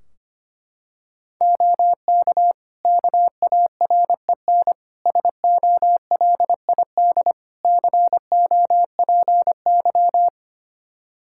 Morse code, heard as a keyed tone, about 25 wpm, 700 Hz. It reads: OK KAREN SOLID COPY
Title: Practice QSO - Exchange 3 - More Info - 25wpm